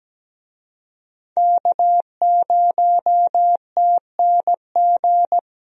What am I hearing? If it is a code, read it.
Morse code, heard as a keyed tone, about 17 wpm, 700 Hz. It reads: K0TNG
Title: USA Call Signs - 5 Characters - Repeated 2x - 17wpm